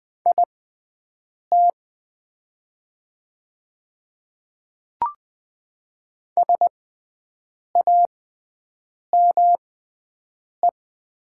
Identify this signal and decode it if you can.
Morse code, heard as a keyed tone, about 20 wpm, spaced as for 7 wpm, 700 Hz. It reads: IT SAME
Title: 6x Character Spacing - Top 200 Words - 20wpm